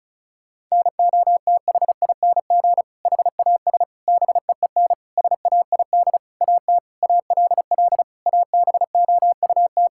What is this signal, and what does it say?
Morse code, heard as a keyed tone, about 36 wpm, 700 Hz. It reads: NOTHING HAS BEEN SAID AT ALL ABOUT
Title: Sentences from Top 1000 Words - Review Part 5 - 35wpm